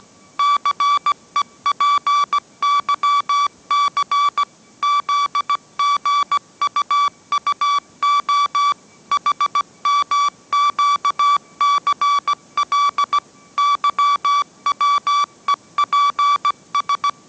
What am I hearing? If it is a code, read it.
Morse code, heard as a keyed tone, about 17 wpm, 1150 Hz. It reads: CEPYC ZGUUO HMQCL YWEPS
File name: mrx / mrx080 18wpm